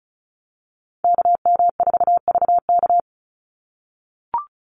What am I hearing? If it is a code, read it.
Morse code, heard as a keyed tone, about 34 wpm, 700 Hz. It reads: KM4VK
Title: USA Call Signs 35wpm